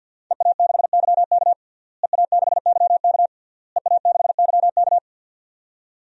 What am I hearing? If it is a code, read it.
Morse code, heard as a keyed tone, about 52 wpm, 700 Hz. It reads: EA6YK EA6YK EA6YK
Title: DXCC-100 Call Signs - 5 Characters - Repeated 3x - 50wpm